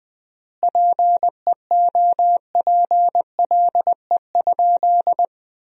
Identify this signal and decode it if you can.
Morse code, heard as a keyed tone, about 20 wpm, 700 Hz. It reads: PEOPLE?
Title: 3x Word-Spacing - Sentences from Top 200 Words - 20wpm